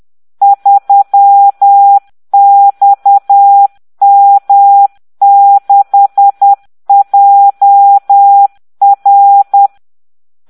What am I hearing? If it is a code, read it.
Morse code, heard as a keyed tone, about 10 wpm, 800 Hz. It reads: 3XM6JR